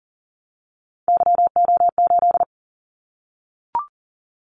Morse code, heard as a keyed tone, about 39 wpm, 700 Hz. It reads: YO8
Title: Callsign Prefixes for Top 25 DXCC - 40wpm